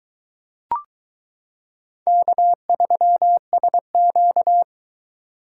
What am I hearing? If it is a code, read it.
Morse code, heard as a keyed tone, about 22 wpm, 700 Hz. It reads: K3SQ